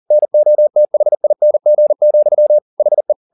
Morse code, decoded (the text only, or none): NOTHING, HE